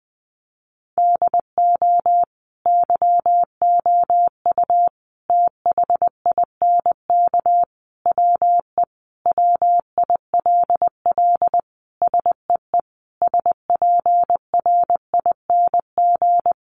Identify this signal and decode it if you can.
Morse code, heard as a keyed tone, about 20 wpm, 700 Hz. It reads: DO YOU THINK WE WILL SEE SPRING
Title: Sentences from Top 1200 Words 20wpm